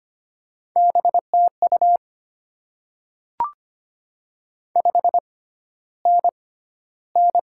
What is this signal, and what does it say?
Morse code, heard as a keyed tone, about 25 wpm, spaced as for 13 wpm, 700 Hz. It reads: BTU 5NN
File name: Top 20 QSO Elements - 6x Character Spacing - 25wpm